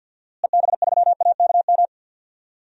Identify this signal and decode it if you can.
Morse code, heard as a keyed tone, about 52 wpm, 700 Hz. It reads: EB3AKM
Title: DXCC-100 Call Signs - 6 Characters - Repeated 3x - 50wpm